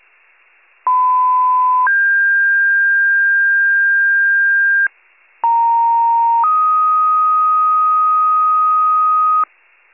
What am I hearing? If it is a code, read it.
Two-tone page, heard as a one-second tone, then a three-second tone, about 1010 then 1600 Hz, about 930 then 1250 Hz, with noise. It is heard from 0.9 to 9.4 s.